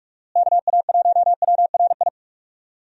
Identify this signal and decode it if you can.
Morse code, heard as a keyed tone, about 43 wpm, 700 Hz. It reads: KA1WRI
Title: USA Call Signs - 6 Characters - Repeated 2x - 45wpm